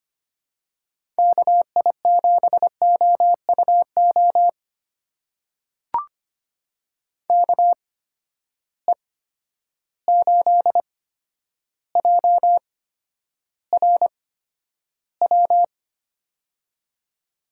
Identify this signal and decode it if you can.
Morse code, heard as a keyed tone, about 25 wpm, spaced as for 11 wpm, 700 Hz. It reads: KI7OUO KE8JRW
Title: USA Call Signs - 6 Characters - 8x Character Spacing - 25wpm